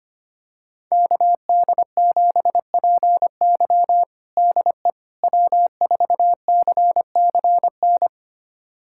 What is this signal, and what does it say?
Morse code, heard as a keyed tone, about 25 wpm, 700 Hz. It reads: KD7PY DE W4CCN
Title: USA Callsign DE Callsign - 5 Characters - 25wpm